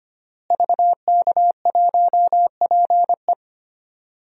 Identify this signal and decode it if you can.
Morse code, heard as a keyed tone, about 25 wpm, 700 Hz. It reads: VK1PE
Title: DXCC-100 Call Signs - 5 Characters - Repeated 3x - 25wpm